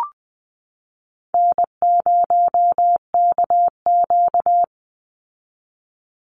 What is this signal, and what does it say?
Morse code, heard as a keyed tone, about 20 wpm, 700 Hz. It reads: N0KQ
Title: Easy Call Signs Repeated 2x - 20wpm